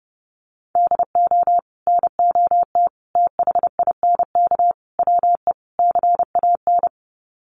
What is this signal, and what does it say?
Morse code, heard as a keyed tone, about 30 wpm, 700 Hz. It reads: DO NOT THINK WE CAN
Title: Sets of 5 Words Encore2 30wpm